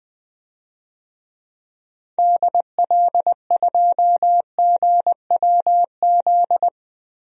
Morse code, heard as a keyed tone, about 20 wpm, 700 Hz. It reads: DL2GWZ
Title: DXCC-100 Call Signs - 6 Characters - 20wpm